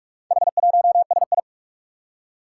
Morse code, heard as a keyed tone, about 45 wpm, 700 Hz. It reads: H1SI
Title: Four Challenging Letter-Number Combinations 45wpm